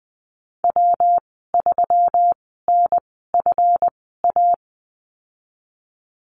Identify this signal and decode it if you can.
Morse code, heard as a keyed tone, about 20 wpm, spaced as for 15 wpm, 700 Hz. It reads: W3NFA
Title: 147 Lesson 24g - Letter y - Call Signs 20wpm